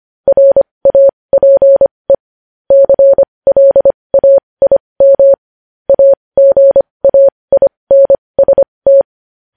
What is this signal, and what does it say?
Morse code, heard as a keyed tone, about 25 wpm, spaced as for 20 wpm, 550 Hz. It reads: RAPE CLAIM AGAINST
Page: News Headlines in Morse Code at 20 WPM